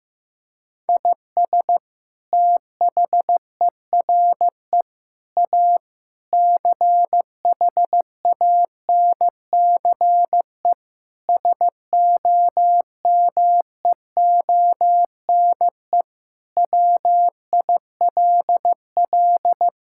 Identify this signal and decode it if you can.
Morse code, heard as a keyed tone, about 15 wpm, 700 Hz. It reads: IS THERE A CHANCE SOMEONE WILL
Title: Sentences from Top 1000 Words - Review Part 6 - 15wpm